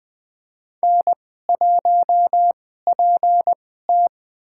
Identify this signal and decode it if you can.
Morse code, heard as a keyed tone, about 20 wpm, spaced as for 14 wpm, 700 Hz. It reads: N1PT
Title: USA Call Signs - 4 Characters - 2x Character Spacing - 20wpm